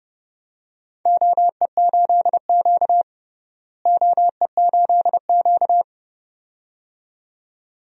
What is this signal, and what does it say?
Morse code, heard as a keyed tone, about 29 wpm, 700 Hz. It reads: OE8Q OE8Q
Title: DXCC-100 Call Signs - 4 Characters - Repeated 2x - 30wpm